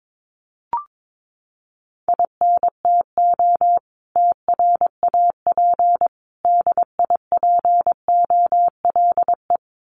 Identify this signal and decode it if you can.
Morse code, heard as a keyed tone, about 22 wpm, 700 Hz. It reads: INTO TRAP DIPOLE
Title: Common QSO Phrases 22wpm